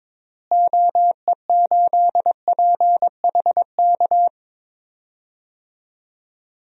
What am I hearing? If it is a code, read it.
Morse code, heard as a keyed tone, about 21 wpm, 700 Hz. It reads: OE8PHK
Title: DXCC-100 Call Signs - 6 Characters - Repeated 3x - 22wpm